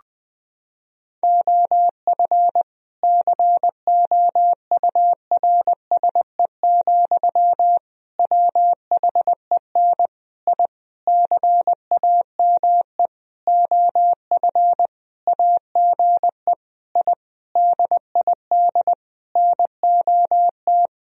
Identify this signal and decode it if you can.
Morse code, heard as a keyed tone, about 20 wpm, 700 Hz. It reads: OF COURSE, WHEN I CAME OF AGE I DID NOT
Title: Sentences from Top 1000 Words - Review Part 3 - 20wpm